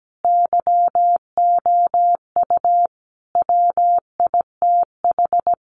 Morse code, heard as a keyed tone, about 17 wpm, 700 Hz. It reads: YOU WITH
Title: Sets of 5 Words Encore2 17wpm